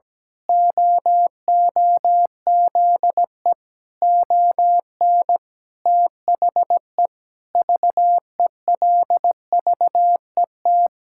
Morse code, heard as a keyed tone, about 17 wpm, 700 Hz. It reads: OOZE ON THE VELVET
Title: Pangram Challenge v2 - 17wpm